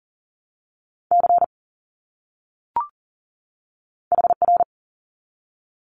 Morse code, heard as a keyed tone, about 40 wpm, spaced as for 6 wpm, 700 Hz. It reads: CHR
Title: Abbreviations 40wpm